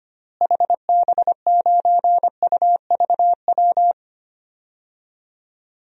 Morse code, heard as a keyed tone, about 25 wpm, 700 Hz. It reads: HB9UVW
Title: DXCC-100 Call Signs - 6 Characters - Repeated 2x - 25wpm